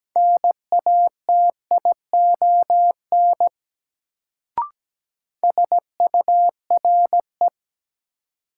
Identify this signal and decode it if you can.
Morse code, heard as a keyed tone, about 17 wpm, 700 Hz. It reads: NATION SURE
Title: Sets of 3 Words - New Word Drill - 17wpm